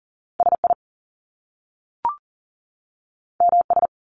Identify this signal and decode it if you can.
Morse code, heard as a keyed tone, about 39 wpm, 700 Hz. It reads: SI MS
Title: Top Two Letter Combinations in Words 40wpm